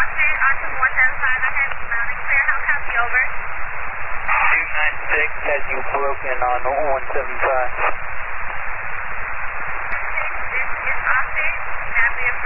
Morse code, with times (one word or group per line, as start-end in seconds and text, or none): none